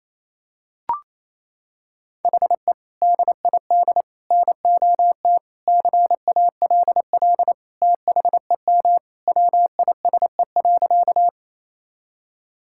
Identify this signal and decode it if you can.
Morse code, heard as a keyed tone, about 28 wpm, 700 Hz. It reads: HE DID NOT CALL THEM WISE.
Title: Sentences from Top 1600 Words 28wpm